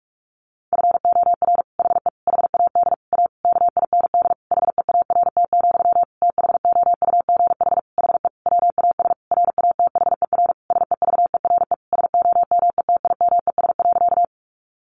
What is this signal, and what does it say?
Morse code, heard as a keyed tone, about 44 wpm, 700 Hz. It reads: FOR HE HAD A KIND HEART, THOUGH HE WAS RATHER SEVERE SOMETIMES.
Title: Sentences from Top 1400 Words 45wpm